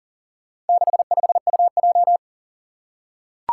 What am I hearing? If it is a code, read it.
Morse code, heard as a keyed tone, about 40 wpm, 700 Hz. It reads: 65UJ
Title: Four Challenging Letter-Number Combinations 40wpm